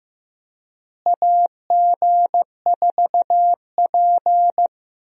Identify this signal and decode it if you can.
Morse code, heard as a keyed tone, about 15 wpm, 700 Hz. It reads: AG4P
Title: USA Call Signs - 4 Characters - Repeated 3x - 15wpm